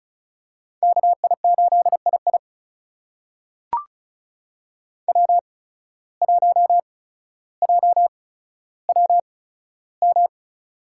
Morse code, heard as a keyed tone, about 36 wpm, spaced as for 15 wpm, 700 Hz. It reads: KI8II W1JWM
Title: USA Call Signs - 8x Character Spacing - 35wpm